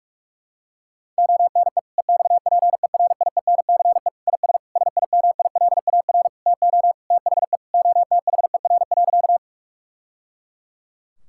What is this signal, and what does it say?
Morse code, heard as a keyed tone, about 46 wpm, 700 Hz. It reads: ONE EXPERIENCE IS SIMILAR TO THE OTHER.
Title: Sentences from Top 1000 Words - Review Part 5 - 45wpm